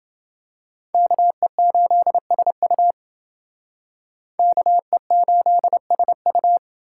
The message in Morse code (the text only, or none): KE8SU KE8SU